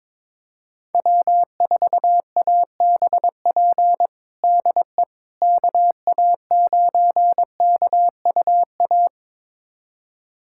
Morse code, heard as a keyed tone, about 22 wpm, 700 Hz. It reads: W4ABP DE KA9KUA
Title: USA Callsign DE Callsign 22wpm